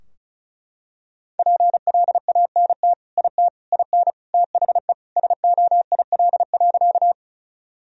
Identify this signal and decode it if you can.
Morse code, heard as a keyed tone, about 36 wpm, 700 Hz. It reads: PLANT IT IN THE SOIL.
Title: Sentences from Top 1000 Words - Review Part 4 - 35wpm